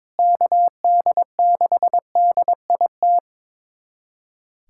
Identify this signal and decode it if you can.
Morse code, heard as a keyed tone, about 22 wpm, 700 Hz. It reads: KD6DIT
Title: USA Callsign DE Callsign - 6 Characters - 22wpm